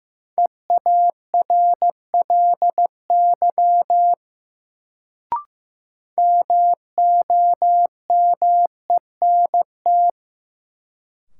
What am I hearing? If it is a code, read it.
Morse code, heard as a keyed tone, about 15 wpm, 700 Hz. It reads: EARLY MOMENT
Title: Sets of 3 Words - New Word Drill - 15wpm